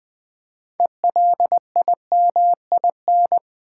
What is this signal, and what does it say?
Morse code, heard as a keyed tone, about 20 wpm, 700 Hz. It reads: ELIMIN
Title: Build A Word - 3 Letters At A Time - 9-Letter Words - 20wpm